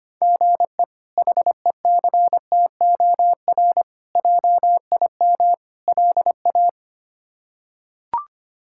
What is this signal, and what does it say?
Morse code, heard as a keyed tone, about 25 wpm, 700 Hz. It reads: GE HECTOR JIM LA
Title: K1USN SST QSO Practice - 25wpm